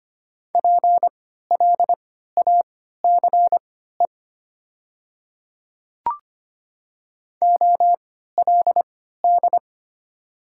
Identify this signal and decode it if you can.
Morse code, heard as a keyed tone, about 25 wpm, spaced as for 14 wpm, 700 Hz. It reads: PLACE OLD
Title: 3x Character Spacing - Top 200 Words - 25wpm